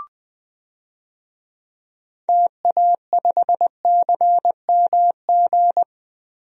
Morse code, heard as a keyed tone, about 20 wpm, 700 Hz. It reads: TA5CMG
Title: DXCC-100 Call Signs - 6 Characters - Repeated 2x - 20wpm